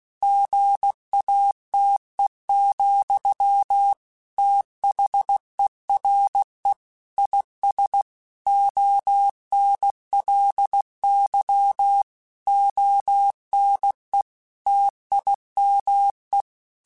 Morse code, heard as a keyed tone, about 16 wpm, 800 Hz. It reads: GATE, THERE IS ONLY ONE TIME
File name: Technical Subjects / Technical Subjects-mp3s / OR Gate @15